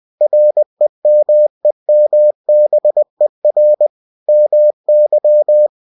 Morse code, heard as a keyed tone, about 20 wpm, 600 Hz. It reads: REMEMBER MY
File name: cw-books / winnie-the-pooh / winnie-20wpm-page-0037